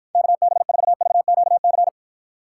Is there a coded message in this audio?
Yes: Morse code from 0.1 to 1.9 s, heard as a keyed tone, about 53 wpm, 700 Hz. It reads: KB4VQC